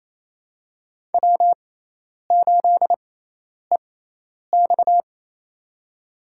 Morse code, heard as a keyed tone, about 28 wpm, spaced as for 10 wpm, 700 Hz. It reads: W8EX
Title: USA Call Signs - 6x Character Spacing - 28wpm